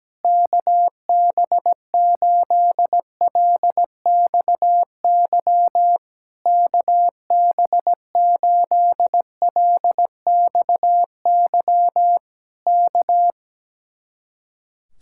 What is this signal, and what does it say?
Morse code, heard as a keyed tone, about 17 wpm, 700 Hz. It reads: KB8LXY KB8LXY K
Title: Speed-Racing Common QSO Phrases - 17wpm